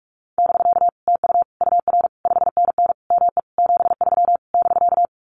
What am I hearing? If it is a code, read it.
Morse code, heard as a keyed tone, about 44 wpm, 700 Hz. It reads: <BK> TU UR 5NN ME 73 <BK>
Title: Common QSO Phrases Encore 45wpm